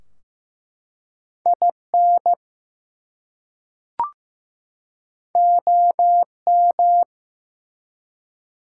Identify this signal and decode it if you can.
Morse code, heard as a keyed tone, about 15 wpm, 700 Hz. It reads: IN OM
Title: Top 180 QSO Elements 15wpm